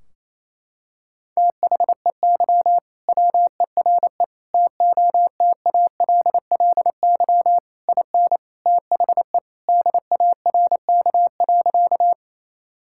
Morse code, heard as a keyed tone, about 28 wpm, 700 Hz. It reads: THEY WERE TOTALLY IN THE DARK.
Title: Sentences from Top 1600 Words 28wpm